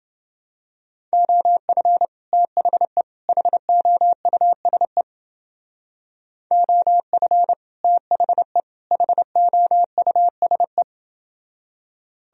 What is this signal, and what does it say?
Morse code, heard as a keyed tone, about 29 wpm, 700 Hz. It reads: OF THE HOUSE OF THE HOUSE